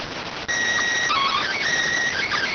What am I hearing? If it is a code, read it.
SSTV (slow-scan television), PD 290.